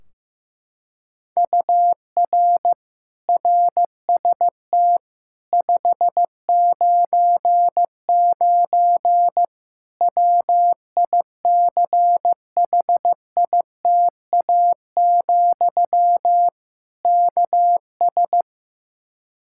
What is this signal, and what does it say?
Morse code, heard as a keyed tone, about 15 wpm, 700 Hz. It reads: UR RST 599 WICHITA, KS
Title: Practice QSO - Exchange 1 - Beginning - 15wpm